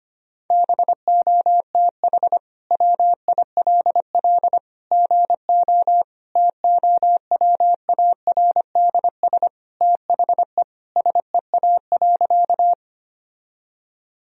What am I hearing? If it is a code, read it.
Morse code, heard as a keyed tone, about 25 wpm, 700 Hz. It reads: BOTH WILL GO TOWARDS THE SEA.
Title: Sentences from Top 1000 Words - Review Part 2 - 25wpm